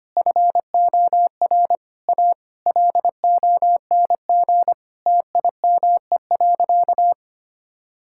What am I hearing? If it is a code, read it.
Morse code, heard as a keyed tone, about 25 wpm, 700 Hz. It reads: FOR A LONG TIME.
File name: practice / 25wpm / Sentences with 4-Word Phrases - 25wpm